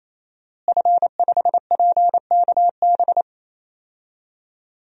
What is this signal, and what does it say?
Morse code, heard as a keyed tone, about 27 wpm, 700 Hz. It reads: F5PKB
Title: DXCC-100 Call Signs - 5 Characters - Repeated 3x - 28wpm